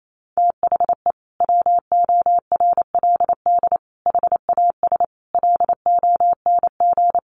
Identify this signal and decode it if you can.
Morse code, heard as a keyed tone, about 28 wpm, 700 Hz. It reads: THE WORLD HAS LONG
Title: Sentences from Top 1200 Words 28wpm